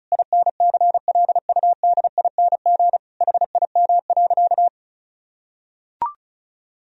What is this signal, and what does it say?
Morse code, heard as a keyed tone, about 36 wpm, 700 Hz. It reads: INCLUDING HIM.
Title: Sentences from Top 1000 Words - Review Part 4 - 35wpm